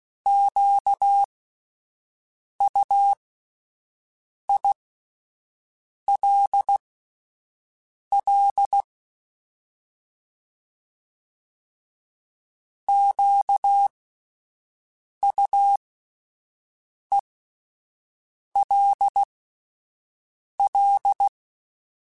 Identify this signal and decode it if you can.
Morse code, heard as a keyed tone, about 16 wpm, spaced as for 5 wpm, 800 Hz. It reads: QUILL QUELL
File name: Alphabetical / Alphabetical mp3s / Qs @5